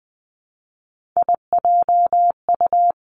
Morse code, heard as a keyed tone, about 20 wpm, 700 Hz. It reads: IJU
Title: Three Challenging Letter-Number Combinations 20wpm